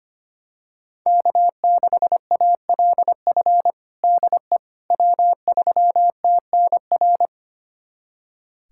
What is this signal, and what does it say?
Morse code, heard as a keyed tone, about 25 wpm, 700 Hz. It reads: K6ALF DE W3TNR
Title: USA Callsign DE Callsign - 5 Characters - 25wpm